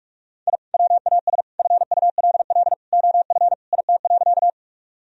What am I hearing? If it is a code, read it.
Morse code, heard as a keyed tone, about 42 wpm, 700 Hz. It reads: I WAS FULL OF IT.
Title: Sentences from Top 1000 Words - Review Part 3 - 45wpm